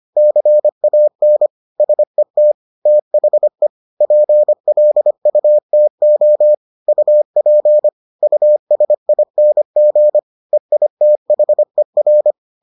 Morse code, heard as a keyed tone, about 25 wpm, 600 Hz. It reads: CAN SET THE PLUTO UP USING EITHER